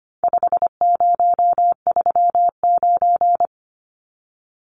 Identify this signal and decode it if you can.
Morse code, heard as a keyed tone, about 25 wpm, 700 Hz. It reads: 5039
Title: Numbers - 4-Digit 25wpm